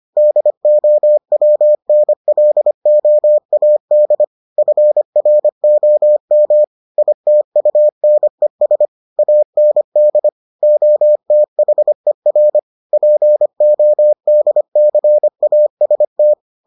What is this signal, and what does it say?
Morse code, heard as a keyed tone, about 25 wpm, 600 Hz. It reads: DOWNLOAD FROM ITUNES AND OTHER PODCAST